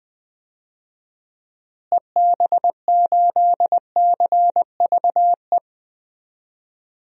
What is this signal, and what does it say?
Morse code, heard as a keyed tone, about 20 wpm, 700 Hz. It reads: EB8CVE
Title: DXCC-100 Call Signs - 6 Characters - Repeated 3x - 20wpm